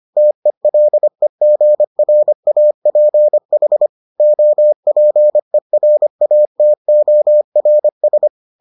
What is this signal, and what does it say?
Morse code, heard as a keyed tone, about 25 wpm, 600 Hz. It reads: TELEGRAPH OPERATORS